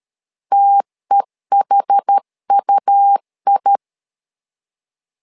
Morse code, heard as a keyed tone, about 13 wpm, 800 Hz. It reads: TEHUI
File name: LEZIONI / 2 UHDN / 2 UH 12wpm 1439